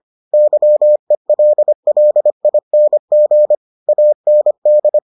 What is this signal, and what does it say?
Morse code, heard as a keyed tone, about 25 wpm, 600 Hz. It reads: YELLING AND